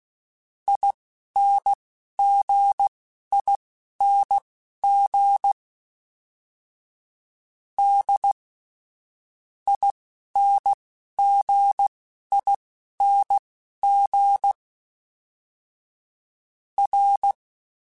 Morse code, heard as a keyed tone, about 16 wpm, spaced as for 11 wpm, 800 Hz. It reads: INGING D INGING R
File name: Special INGs / Special INGs-mp3s / Dble Ings sp @10